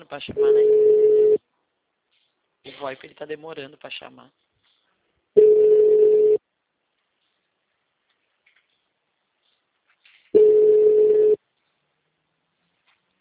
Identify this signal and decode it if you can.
Ringback tone (Europe, 425 Hz), with speech.
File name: out- -2036- - - .4433